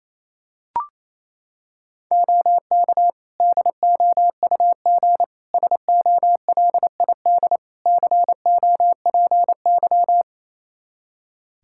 Morse code, heard as a keyed tone, about 28 wpm, 700 Hz. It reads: OK DOUG SOLID COPY
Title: Practice QSO - Exchange 3 - More Info - 28wpm